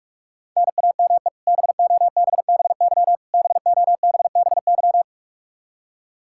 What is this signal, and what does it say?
Morse code, heard as a keyed tone, about 41 wpm, 700 Hz. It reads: NAME BOBBY BOBBY
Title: Practice QSO - Exchange 1 - Beginning - 45wpm